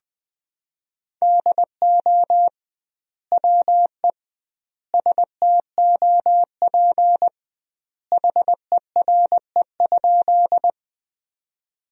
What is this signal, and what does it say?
Morse code, heard as a keyed tone, about 21 wpm, 700 Hz. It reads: DO WE STOP HERE?
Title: 2x Word-Spacing - Sentences from Top 300 Words - 20wpm